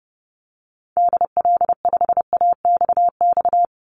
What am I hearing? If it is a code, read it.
Morse code, heard as a keyed tone, about 30 wpm, 700 Hz. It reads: DL5AXX
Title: Call Signs - 30wpm